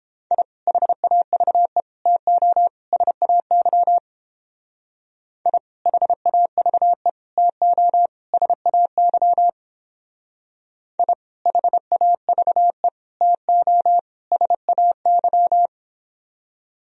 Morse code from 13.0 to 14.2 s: TO

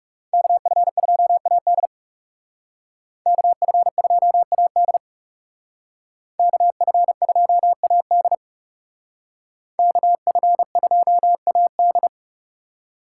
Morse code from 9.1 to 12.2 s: KF2AD

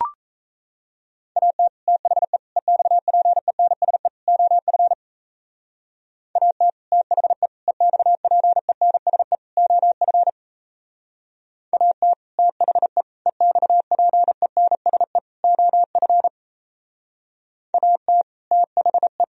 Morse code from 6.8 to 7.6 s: THE

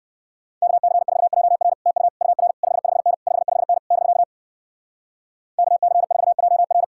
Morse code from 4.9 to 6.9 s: KC4JR

